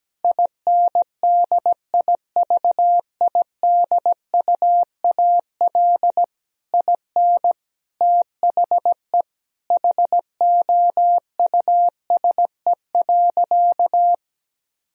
Morse code, heard as a keyed tone, about 17 wpm, 700 Hz. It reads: INDIVIDUAL IN THE HOUSE.
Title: Sentences from Top 1000 Words - Review Part 4 - 17wpm